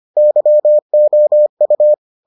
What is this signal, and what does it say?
Morse code, heard as a keyed tone, about 25 wpm, 600 Hz. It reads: YOU